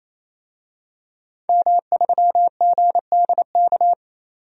Morse code, heard as a keyed tone, about 28 wpm, 700 Hz. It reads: M3GDK
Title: DXCC-100 Call Signs - 5 Characters - 28wpm